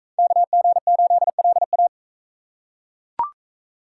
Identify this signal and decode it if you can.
Morse code, heard as a keyed tone, about 39 wpm, 700 Hz. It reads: KG8LA